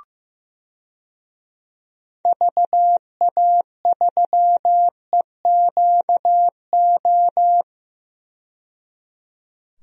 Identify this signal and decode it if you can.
Morse code, heard as a keyed tone, about 15 wpm, 700 Hz. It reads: VA3EQO
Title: DXCC-100 Call Signs - 6 Characters - 15wpm